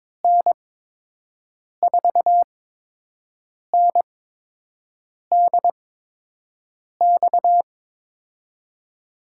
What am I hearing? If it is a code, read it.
Morse code, heard as a keyed tone, about 23 wpm, spaced as for 6 wpm, 700 Hz. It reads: N4NDX
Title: USA Call Signs - 8x Character Spacing - 22wpm